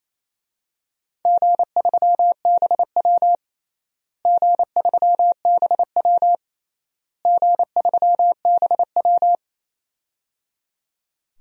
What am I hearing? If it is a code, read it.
Morse code, heard as a keyed tone, about 28 wpm, 700 Hz. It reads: G3BW G3BW G3BW
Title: DXCC-100 Call Signs - 4 Characters - Repeated 3x - 28wpm